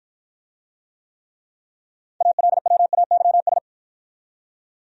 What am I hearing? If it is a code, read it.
Morse code, heard as a keyed tone, about 53 wpm, 700 Hz. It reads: ALWAYS